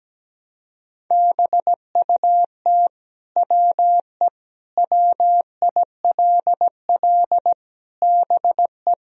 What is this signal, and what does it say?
Morse code, heard as a keyed tone, about 17 wpm, 700 Hz. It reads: BUT WE WILL BE
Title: Sentences from Top 1000 Words - Review Part 3 - 17wpm